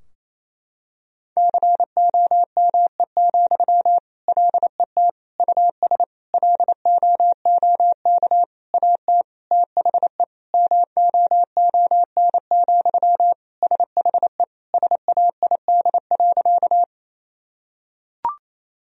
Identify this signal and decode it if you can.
Morse code, heard as a keyed tone, about 29 wpm, 700 Hz. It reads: COME, LET US LOOK AT THE MOON, SHE SAID.
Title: Sentences from Top 1000 Words - Review Part 3 - 28wpm